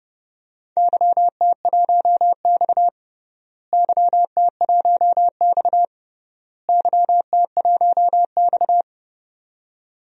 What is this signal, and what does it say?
Morse code, heard as a keyed tone, about 31 wpm, 700 Hz. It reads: YT1X YT1X YT1X
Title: DXCC-100 Call Signs - 4 Characters - Repeated 3x - 30wpm